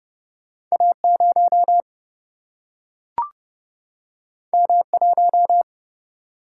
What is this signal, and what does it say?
Morse code, heard as a keyed tone, about 30 wpm, 700 Hz. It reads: A0 M1
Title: Single-Letter Single-Number 30wpm